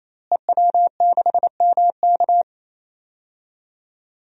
Morse code, heard as a keyed tone, about 28 wpm, 700 Hz. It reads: EW6MK
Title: DXCC-100 Call Signs - 5 Characters - Repeated 3x - 28wpm